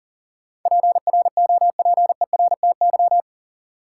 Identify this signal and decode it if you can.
Morse code, heard as a keyed tone, about 40 wpm, 700 Hz. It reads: PROPERTY
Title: Sentences from Top 1000 Words - New Word Drill - 40wpm